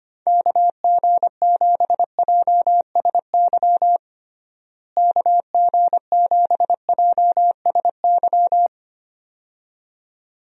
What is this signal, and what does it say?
Morse code, heard as a keyed tone, about 25 wpm, 700 Hz. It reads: KG7JSY KG7JSY
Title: USA Call Signs - 6 Characters - Repeated 2x - 25wpm